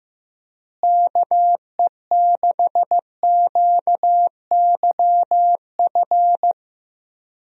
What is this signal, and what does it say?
Morse code, heard as a keyed tone, about 15 wpm, 700 Hz. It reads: KE6QYF
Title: USA Call Signs - 6 Characters - Repeated 3x - 15wpm